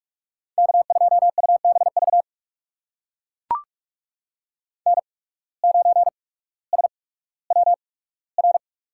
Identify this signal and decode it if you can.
Morse code, heard as a keyed tone, about 43 wpm, spaced as for 19 wpm, 700 Hz. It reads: K2UBV N9SWR
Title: USA Call Signs - 5 Characters - 8x Character Spacing - 45wpm